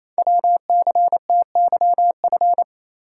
Morse code, heard as a keyed tone, about 29 wpm, 700 Hz. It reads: WCTYF
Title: Code Groups - 5 Letter:Numbers - 28wpm